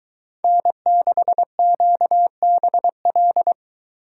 Morse code, heard as a keyed tone, about 22 wpm, 700 Hz. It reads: N6QBL